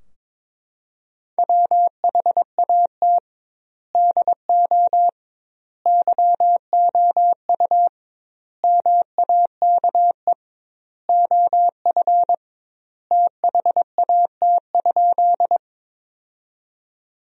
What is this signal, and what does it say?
Morse code, heard as a keyed tone, about 22 wpm, 700 Hz. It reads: WHAT DO YOU MAKE OF THAT?
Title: 2x Word-Spacing - Sentences from Top 100 Words - 22wpm